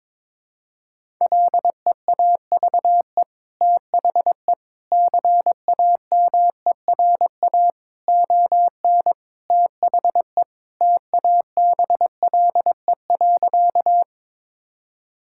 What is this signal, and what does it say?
Morse code, heard as a keyed tone, about 22 wpm, 700 Hz. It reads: LEAVE THE CAMERA ON THE TABLE.
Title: Sentences from Top 1000 Words - Review Part 5 - 22wpm